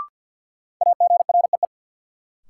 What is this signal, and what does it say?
Morse code, heard as a keyed tone, about 53 wpm, 700 Hz. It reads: AGREE
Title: Sets of 5 Words Encore - New Word Drill - 50wpm